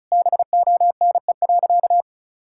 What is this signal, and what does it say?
Morse code, heard as a keyed tone, about 36 wpm, 700 Hz. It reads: BONE.
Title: Sentences from Top 1600 Words 35wpm